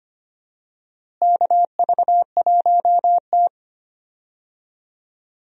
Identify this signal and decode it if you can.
Morse code, heard as a keyed tone, about 25 wpm, 700 Hz. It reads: KV1T
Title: USA Call Signs - 4 Characters - Repeated 3x - 25wpm